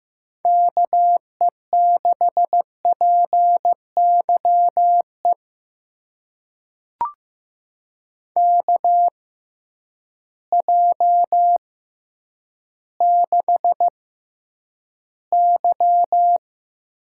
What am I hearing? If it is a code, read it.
Morse code, heard as a keyed tone, about 15 wpm, spaced as for 9 wpm, 700 Hz. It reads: KE6PYE KJ6Y
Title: USA Call Signs - 6 Characters - 6x Character Spacing - 15wpm